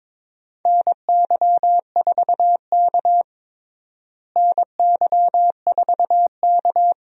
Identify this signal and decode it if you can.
Morse code, heard as a keyed tone, about 22 wpm, 700 Hz. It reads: NY4K NY4K
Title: USA Call Signs - 4 Characters - Repeated 3x - 22wpm